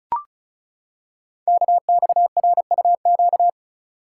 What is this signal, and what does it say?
Morse code, heard as a keyed tone, about 36 wpm, 700 Hz. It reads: KXRUQ
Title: Code Groups - 5 Letters - 35wpm